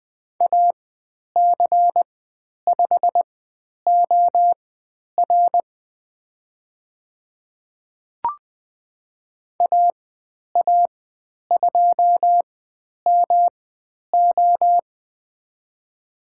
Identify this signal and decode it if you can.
Morse code, heard as a keyed tone, about 20 wpm, spaced as for 10 wpm, 700 Hz. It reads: AC5OR AA2MO
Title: 109 Lesson 18g - Letter m - Call Signs 20wpm